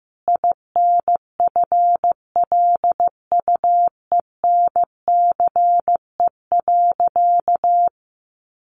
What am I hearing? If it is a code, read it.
Morse code, heard as a keyed tone, about 15 wpm, 700 Hz. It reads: INFLUENCE.
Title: Sentences from Top 1100 Words 15wpm